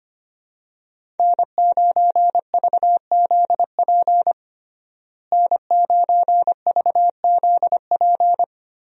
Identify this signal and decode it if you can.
Morse code, heard as a keyed tone, about 25 wpm, 700 Hz. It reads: N9VZP N9VZP
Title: USA Call Signs - 5 Characters - Repeated 2x - 25wpm